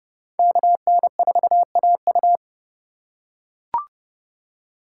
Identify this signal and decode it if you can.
Morse code, heard as a keyed tone, about 30 wpm, 700 Hz. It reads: KN4AU